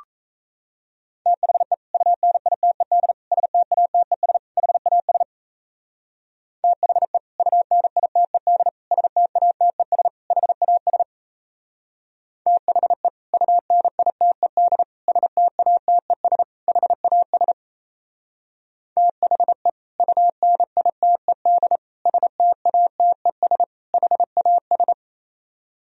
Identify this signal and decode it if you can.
Morse code, heard as a keyed tone, about 34 wpm, 700 Hz. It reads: THE UNITED STATES HAS THE UNITED STATES HAS THE UNITED STATES HAS THE UNITED STATES HAS
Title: Speed-Racing - Sets of 4 Words 28wpm